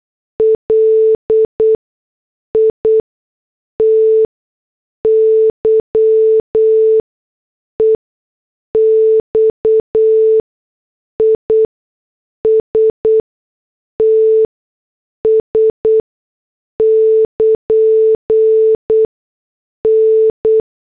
Morse code, heard as a keyed tone, about 8 wpm, spaced as for 6 wpm, 440 Hz. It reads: LITY EXISTS <KN> N
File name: morse / headlines